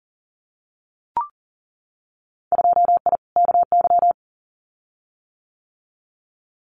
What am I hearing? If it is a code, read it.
Morse code, heard as a keyed tone, about 40 wpm, 700 Hz. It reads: 2I KY
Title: Winter Field Day Exchanges 40wpm